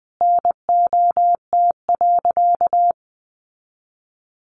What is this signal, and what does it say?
Morse code, heard as a keyed tone, about 20 wpm, 700 Hz. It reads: NOT.
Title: Sentences from Top 900 Words 20wpm